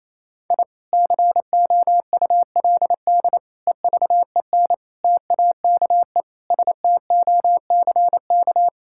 Morse code, heard as a keyed tone, about 28 wpm, 700 Hz. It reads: I COULD EVEN TAKE STOCK
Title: Sentences from Top 1000 Words - Review Part 5 - 28wpm